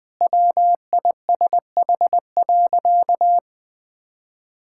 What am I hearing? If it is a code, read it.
Morse code, heard as a keyed tone, about 20 wpm, 700 Hz. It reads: WISH.
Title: Sentences with 2-Word Phrases - Spoken Beginning - 20wpm